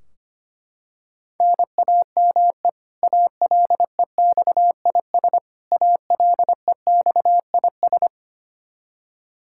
Morse code, heard as a keyed tone, about 25 wpm, 700 Hz. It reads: NAME ALEXIS ALEXIS
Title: Practice QSO - Exchange 1 - Beginning - 25wpm